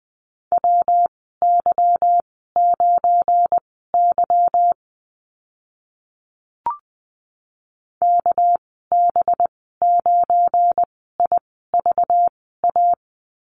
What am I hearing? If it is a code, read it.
Morse code, heard as a keyed tone, about 20 wpm, spaced as for 14 wpm, 700 Hz. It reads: WY9Y KB9IVA
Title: 238 Lesson 38g - Letter q - Call Signs 20wpm